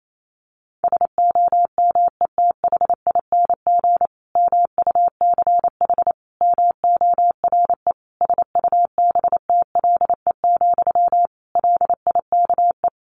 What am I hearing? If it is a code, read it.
Morse code, heard as a keyed tone, about 28 wpm, 700 Hz. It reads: SOMETHING MUCH MORE SUBTLE, LIKE
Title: A Sound Of Thunder 28wpm